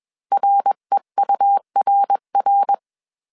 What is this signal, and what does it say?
Morse code, heard as a keyed tone, about 21 wpm, 800 Hz. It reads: REURR
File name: CW / DET / 1 3 1500C S B 20wpm 1754